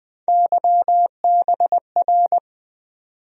Morse code, heard as a keyed tone, about 20 wpm, 700 Hz. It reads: YBR
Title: USA Call Sign Postfixes - 3 Letters 20wpm